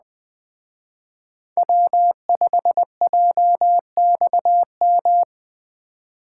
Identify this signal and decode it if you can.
Morse code, heard as a keyed tone, about 20 wpm, 700 Hz. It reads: W5JXM
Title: Speed Racing - USA 1x3 Call Signs 17wpm